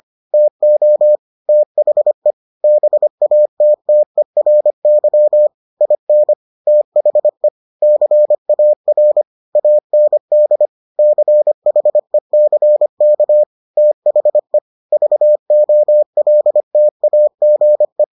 Morse code, heard as a keyed tone, about 25 wpm, 600 Hz. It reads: TO THE BATTERY IN THE CAR AND CHECK THE VOLTAGE